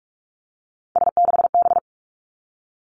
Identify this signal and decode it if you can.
Morse code, heard as a keyed tone, about 46 wpm, 700 Hz. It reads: S6B
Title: Three Challenging Letter-Number Combinations 45wpm